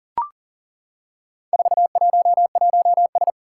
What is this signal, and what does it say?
Morse code, heard as a keyed tone, about 40 wpm, 700 Hz. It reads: 411S
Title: Four Challenging Letter-Number Combinations 40wpm